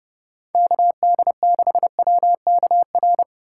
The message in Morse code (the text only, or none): KD6WKR